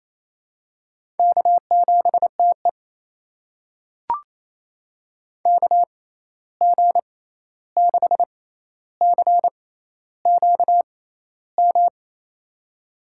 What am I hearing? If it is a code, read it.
Morse code, heard as a keyed tone, about 28 wpm, spaced as for 13 wpm, 700 Hz. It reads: K7TE KG6CQM
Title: USA Call Signs - 6x Character Spacing - 28wpm